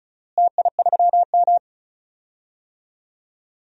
Morse code, heard as a keyed tone, about 36 wpm, 700 Hz. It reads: TI3M
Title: DXCC-100 Call Signs - 4 Characters - Repeated 2x - 35wpm